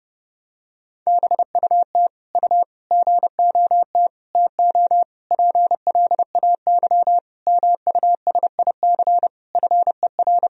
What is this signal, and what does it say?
Morse code, heard as a keyed tone, about 30 wpm, 700 Hz. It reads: BUT U GOT TO PLAY MUSIC FER
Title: Sentences from Top 300 Words and Abbreviations 30wpm